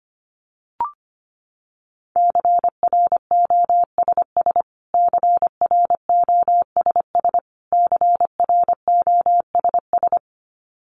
Morse code, heard as a keyed tone, about 25 wpm, 700 Hz. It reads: CROSS CROSS CROSS
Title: Top 500 Words Repeated 25wpm